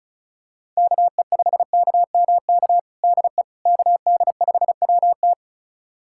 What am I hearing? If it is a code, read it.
Morse code, heard as a keyed tone, about 36 wpm, 700 Hz. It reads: KE5KMK DE KD5WT
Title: USA Callsign DE Callsign 35wpm